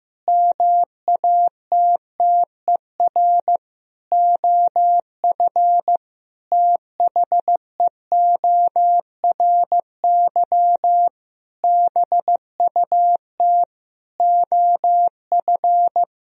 Morse code, heard as a keyed tone, about 15 wpm, 700 Hz. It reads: MATTER OF THEORY BUT OF
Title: Sentences from Top 1000 Words - Review Part 4 - 15wpm